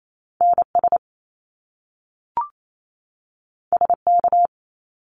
Morse code, heard as a keyed tone, about 28 wpm, 700 Hz. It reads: NS SK
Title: Canadian Provinces and Territories 28wpm